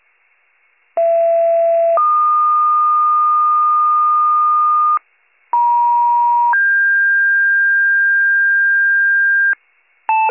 Two-tone page, heard as a one-second tone, then a three-second tone, about 670 then 1220 Hz, about 950 then 1600 Hz, with noise. It is heard from 1.0 to 9.5 s.